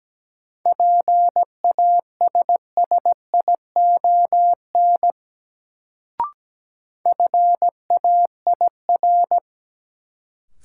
Morse code, heard as a keyed tone, about 17 wpm, 700 Hz. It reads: PASSION FAIR
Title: Sentences from Top 1300 Words - New Word Drill Alt - 17wpm